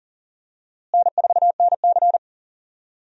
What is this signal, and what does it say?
Morse code, heard as a keyed tone, about 40 wpm, 700 Hz. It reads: N4NC
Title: USA Call Signs - 4 Characters - Repeated 3x - 40wpm